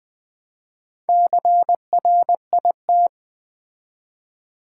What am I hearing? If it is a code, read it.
Morse code, heard as a keyed tone, about 20 wpm, 700 Hz. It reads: CRIT
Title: Build A Word - 2 Letters At A Time - 6-Letter Words - 20wpm